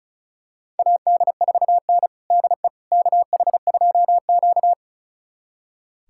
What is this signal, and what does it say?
Morse code, heard as a keyed tone, about 36 wpm, 700 Hz. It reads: AD4N DE KH2Q
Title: USA Callsign DE Callsign - 4 Characters - 35wpm